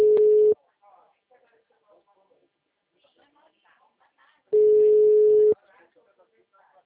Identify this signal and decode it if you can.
Ringback tone (Europe, 425 Hz), with speech.
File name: out- -2017- - - .99405